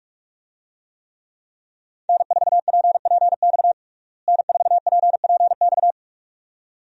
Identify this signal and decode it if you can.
Morse code, heard as a keyed tone, about 41 wpm, 700 Hz. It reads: N4PPX N4PPX
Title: USA Call Signs - 5 Characters - Repeated 2x - 45wpm